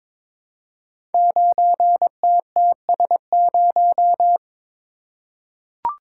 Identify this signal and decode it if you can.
Morse code, heard as a keyed tone, about 22 wpm, 700 Hz. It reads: 9TTS0
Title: Code Groups - 5 Letter:Numbers - 22wpm